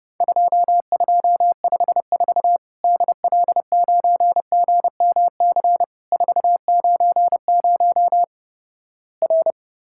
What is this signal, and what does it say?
Morse code, heard as a keyed tone, about 31 wpm, 700 Hz. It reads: 2254 DL9GMC 490 R
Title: Worked All Europe DX Contest - QTC Exchange - 30wpm